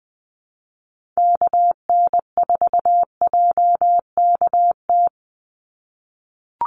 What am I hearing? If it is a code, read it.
Morse code, heard as a keyed tone, about 20 wpm, 700 Hz. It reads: KN4JKT
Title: USA Call Signs 20wpm